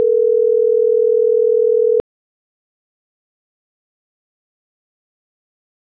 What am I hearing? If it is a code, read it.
Ringback tone (North America), 440 + 480 Hz.